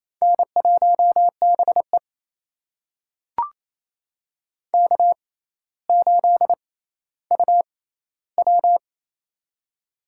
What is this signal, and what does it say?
Morse code, heard as a keyed tone, about 29 wpm, spaced as for 15 wpm, 700 Hz. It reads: N1BE K8UW
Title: USA Call Signs - 4 Characters - 6x Character Spacing - 28wpm